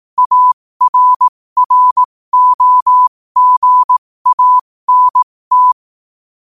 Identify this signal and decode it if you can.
Morse code, heard as a keyed tone, about 18 wpm, spaced as for 15 wpm, 1000 Hz. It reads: ARROGANT
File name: qotdfiles / 15wpm